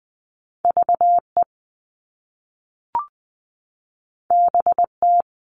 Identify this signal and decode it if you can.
Morse code, heard as a keyed tone, about 20 wpm, 700 Hz. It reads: VE BT
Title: All Cut Numbers - 2-Digit 20wpm